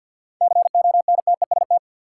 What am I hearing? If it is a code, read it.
Morse code, heard as a keyed tone, about 46 wpm, 700 Hz. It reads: CONTEST
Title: Sentences from Top 1600 Words - New Word Drill Alt - 50wpm